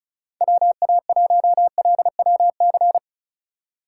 Morse code, heard as a keyed tone, about 34 wpm, 700 Hz. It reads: WA1LWC
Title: K1USN SST QSO Practice - 35wpm